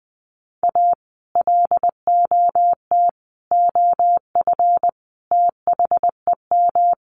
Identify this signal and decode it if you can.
Morse code, heard as a keyed tone, about 20 wpm, 700 Hz. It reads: A LOT OF THEM
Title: Sets of 5 Words Encore2 20wpm